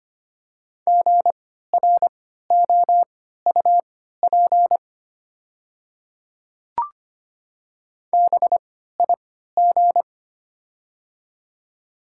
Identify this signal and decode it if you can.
Morse code, heard as a keyed tone, about 25 wpm, spaced as for 14 wpm, 690 Hz. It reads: GROUP BIG
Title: 3x Character Spacing - Top 300 Words - 25wpm